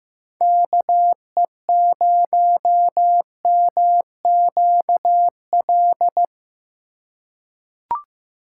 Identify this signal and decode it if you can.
Morse code, heard as a keyed tone, about 15 wpm, 700 Hz. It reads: KE0MQL
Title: USA Call Signs - 6 Characters - 4x Character Spacing - 15wpm